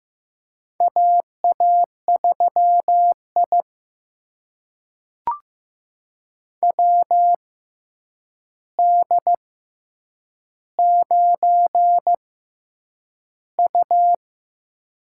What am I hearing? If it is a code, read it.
Morse code, heard as a keyed tone, about 15 wpm, spaced as for 8 wpm, 700 Hz. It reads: AA3I WD9U
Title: USA Call Signs - 4 Characters - 6x Character Spacing - 15wpm